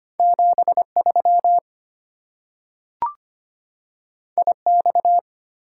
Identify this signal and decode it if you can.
Morse code, heard as a keyed tone, about 25 wpm, 690 Hz. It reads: 73 IX
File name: Two Letter-Number Combinations - 25wpm